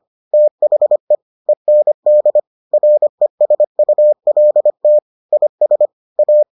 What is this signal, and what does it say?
Morse code, heard as a keyed tone, about 25 wpm, 600 Hz. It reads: THE END RESULT IS A